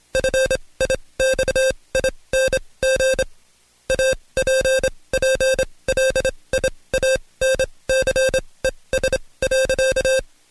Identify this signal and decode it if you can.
Morse code, heard as a keyed tone, about 26 wpm, spaced as for 20 wpm, 530 Hz. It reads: FIXING APPLIANCES.
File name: Code Quick / AudioDownld2 / Seg136-Stepper9 19WPM